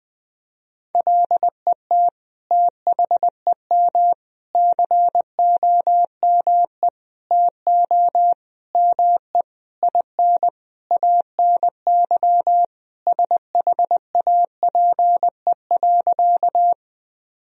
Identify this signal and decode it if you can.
Morse code, heard as a keyed tone, about 20 wpm, 700 Hz. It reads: LET THEM COME TO ME IN ANY SHAPE.
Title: Sentences from Top 1000 Words - Review Part 3 - 20wpm